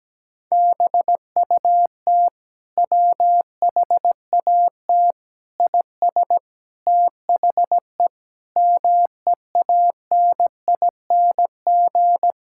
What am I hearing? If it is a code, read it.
Morse code, heard as a keyed tone, about 17 wpm, 700 Hz. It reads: BUT WHAT IS THE MEANING
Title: Sentences from Top 1000 Words - Review Part 6 - 17wpm